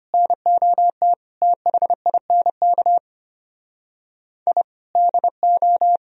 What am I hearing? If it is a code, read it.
Morse code, heard as a keyed tone, about 32 wpm, 700 Hz. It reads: NOT THINK I DO